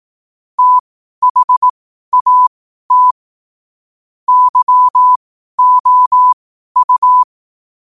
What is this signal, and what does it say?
Morse code, heard as a keyed tone, about 18 wpm, spaced as for 12 wpm, 1000 Hz. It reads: THAT YOU
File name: qotdfiles / 12wpm